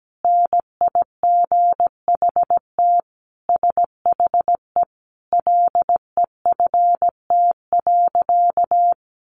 Morse code, heard as a keyed tone, about 17 wpm, 700 Hz. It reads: NIGHT SHE LEFT.
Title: Sentences from Top 300 Words 17wpm